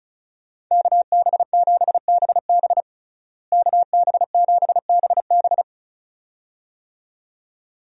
Morse code, heard as a keyed tone, about 34 wpm, 700 Hz. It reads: KB7BB KB7BB
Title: USA Call Signs - 5 Characters - Repeated 2x - 35wpm